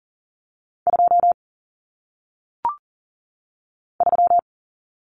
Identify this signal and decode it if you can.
Morse code, heard as a keyed tone, about 40 wpm, 700 Hz. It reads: 2 3
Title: Common Cut Numbers - 1-Digit 40wpm